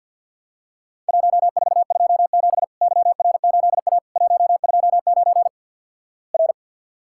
Morse code, heard as a keyed tone, about 50 wpm, 700 Hz. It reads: 1427 YR8A 129 R
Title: Worked All Europe DX Contest - QTC Exchange - 50wpm